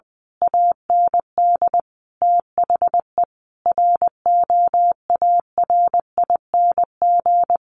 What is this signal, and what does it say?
Morse code, heard as a keyed tone, about 20 wpm, 700 Hz. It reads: AND THE ROARING
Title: A Sound Of Thunder 20wpm